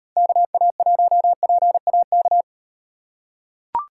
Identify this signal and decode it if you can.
Morse code, heard as a keyed tone, about 36 wpm, 700 Hz. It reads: KA1PAK